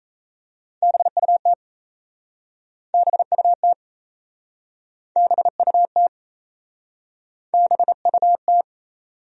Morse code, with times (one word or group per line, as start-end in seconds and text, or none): 0.8-1.5 BUT
2.9-3.7 BUT
5.2-6.1 BUT
7.5-8.6 BUT